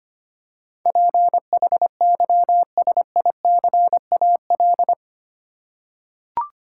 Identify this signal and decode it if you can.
Morse code, heard as a keyed tone, about 25 wpm, 700 Hz. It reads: PHYSICAL
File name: Eight Letter Words - 25wpm